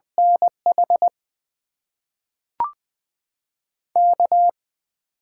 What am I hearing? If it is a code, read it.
Morse code, heard as a keyed tone, about 20 wpm, 700 Hz. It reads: NH K
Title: USA Call Signs - 6 Characters - 8x Character Spacing - 20wpm